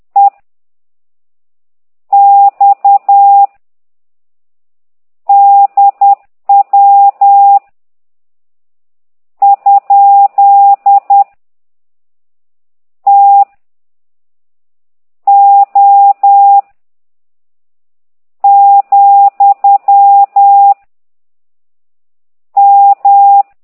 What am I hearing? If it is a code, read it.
Morse code, heard as a keyed tone, about 10 wpm, 800 Hz. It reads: E X D W ? T O , M
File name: ct2idw / exerc15